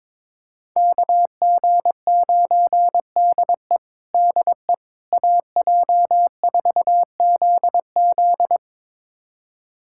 Morse code, heard as a keyed tone, about 21 wpm, 700 Hz. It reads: KG9DE DE AJ4ZZ
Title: USA Callsign DE Callsign - 5 Characters - 22wpm